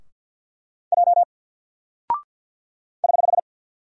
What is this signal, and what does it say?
Morse code, heard as a keyed tone, about 48 wpm, 700 Hz. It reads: J <HH>
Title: Single Character (Letter:Number:Punctuation:Prosign) Alt 50wpm